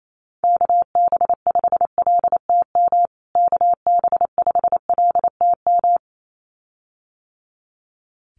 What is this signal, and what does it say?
Morse code, heard as a keyed tone, about 28 wpm, 700 Hz. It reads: KB5LTM KB5LTM
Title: USA Call Signs Repeated 2x 28wpm